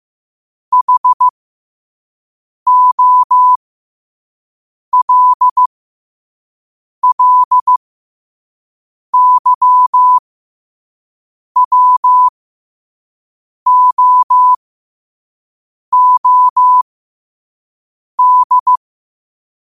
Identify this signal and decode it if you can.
Morse code, heard as a keyed tone, about 15 wpm, spaced as for 5 wpm, 1000 Hz. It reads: HOLLYWOOD